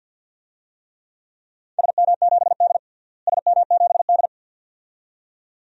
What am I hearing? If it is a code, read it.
Morse code, heard as a keyed tone, about 48 wpm, 700 Hz. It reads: SM7D SM7D
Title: DXCC-100 Call Signs - 4 Characters - Repeated 2x - 50wpm